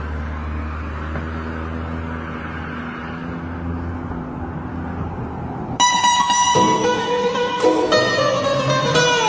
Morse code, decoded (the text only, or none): none